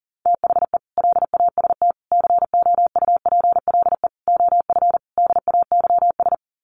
Morse code, heard as a keyed tone, about 40 wpm, 700 Hz. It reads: THE LAST COUPLE OF DAYS
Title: Sets of 5 Words Encore 40wpm